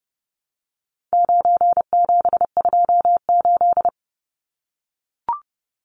Morse code, heard as a keyed tone, about 30 wpm, 700 Hz. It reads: 9728
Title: Numbers - 4-Digit 30wpm